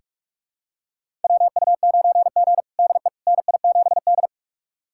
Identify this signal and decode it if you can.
Morse code, heard as a keyed tone, about 41 wpm, 700 Hz. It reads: WU9G DE NI7D
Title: USA Callsign DE Callsign - 4 Characters - 45wpm